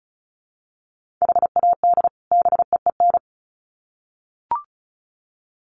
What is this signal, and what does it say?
Morse code, heard as a keyed tone, about 34 wpm, 700 Hz. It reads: HAD BEEN
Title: Sets of 2 Words 35wpm